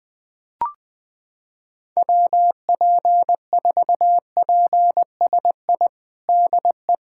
Morse code, heard as a keyed tone, about 20 wpm, 700 Hz. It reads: WP4PSI DE
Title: USA Callsign DE Callsign - 6 Characters - 20wpm